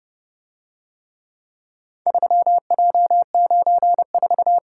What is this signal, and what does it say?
Morse code, heard as a keyed tone, about 29 wpm, 700 Hz. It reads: 3J94